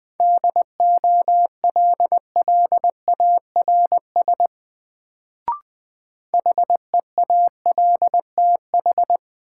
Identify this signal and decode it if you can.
Morse code, heard as a keyed tone, about 20 wpm, 700 Hz. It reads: DOLLARS HEALTH
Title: Sets of 5 Words - New Word Drill - 20wpm